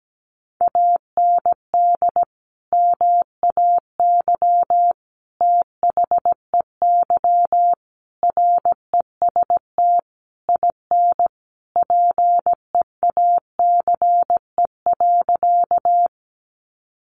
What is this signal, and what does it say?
Morse code, heard as a keyed tone, about 17 wpm, 700 Hz. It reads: AND MAY THEY REST IN PEACE.
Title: Sentences from Top 1500 Words 17wpm